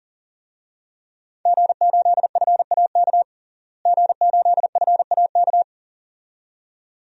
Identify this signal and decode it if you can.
Morse code, heard as a keyed tone, about 40 wpm, 700 Hz. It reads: G8FAK G8FAK
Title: DXCC-100 Call Signs - 5 Characters - Repeated 2x - 40wpm